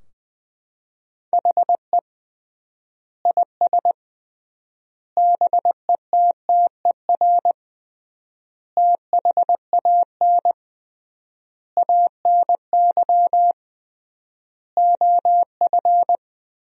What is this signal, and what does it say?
Morse code, heard as a keyed tone, about 20 wpm, 700 Hz. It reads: HE IS BETTER THAN ANY OF
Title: 3x Word-Spacing - Sentences from Top 400 Words - 20wpm